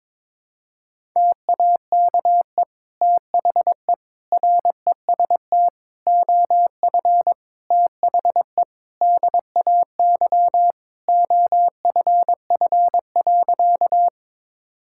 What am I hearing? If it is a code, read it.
Morse code, heard as a keyed tone, about 22 wpm, 700 Hz. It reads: TAKE THE REST OF THE DAY OFF.
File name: practice / 22wpm / Sentences with 2-Word Phrases - 22wpm